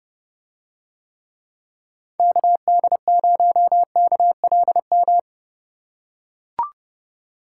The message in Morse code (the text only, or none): KD0KLM